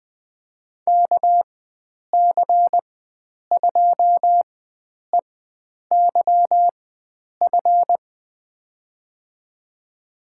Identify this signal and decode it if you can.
Morse code, heard as a keyed tone, about 20 wpm, spaced as for 9 wpm, 700 Hz. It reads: KC2EYF
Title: USA Call Signs - 6 Characters - 4x Character Spacing - 20wpm